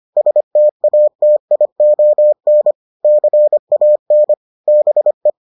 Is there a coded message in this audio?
Yes: Morse code from 0.2 to 5.3 s, heard as a keyed tone, about 25 wpm, 600 Hz. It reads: STATION CAN BE